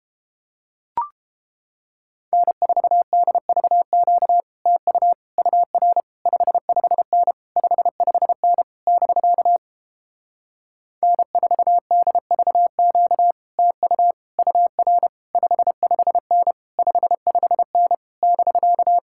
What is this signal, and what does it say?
Morse code, heard as a keyed tone, about 31 wpm, 700 Hz. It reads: N4DVQ TU UR 55N 55N <BK> N4DVQ TU UR 55N 55N <BK>